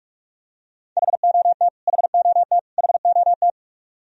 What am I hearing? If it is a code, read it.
Morse code, heard as a keyed tone, about 46 wpm, 700 Hz. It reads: HOT HOT HOT
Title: Top 100 Words Repeated 45wpm